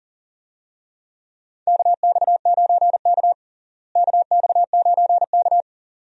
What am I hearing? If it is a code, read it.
Morse code, heard as a keyed tone, about 40 wpm, 700 Hz. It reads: KX9K KX9K
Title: USA Call Signs - 4 Characters - Repeated 2x - 40wpm